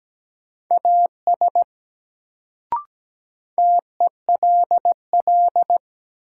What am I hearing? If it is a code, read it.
Morse code, heard as a keyed tone, about 17 wpm, 700 Hz. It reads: AS TELL
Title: Sets of 2 Words - New Word Drill - 17wpm